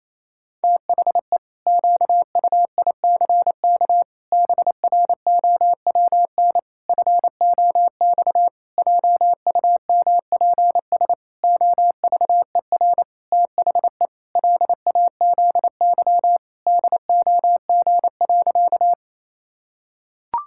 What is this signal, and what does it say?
Morse code, heard as a keyed tone, about 28 wpm, 700 Hz. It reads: THE QUICK BROWN FOX JUMPS OVER THE LAZY DOG.
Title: Pangram Challenge v2 - 28wpm